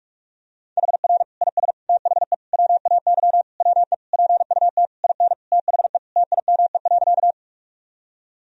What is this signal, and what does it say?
Morse code, heard as a keyed tone, about 46 wpm, 700 Hz. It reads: HR IS THE WAY WE PUT IN THE TIME.
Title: Sentences from Top 200 Words and Abbreviations 45wpm